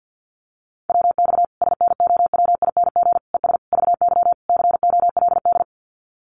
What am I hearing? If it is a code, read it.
Morse code, heard as a keyed tone, about 50 wpm, 700 Hz. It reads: WX SNOWING ES VY COLD
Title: Common QSO Phrases 50wpm